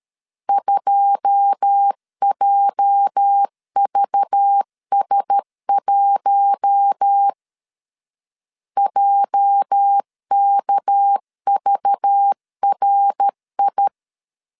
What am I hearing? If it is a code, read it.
Morse code, heard as a keyed tone, about 13 wpm, 800 Hz. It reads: 2JVS1 JKVRI
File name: LEZIONI / 9 286 12wpm 2305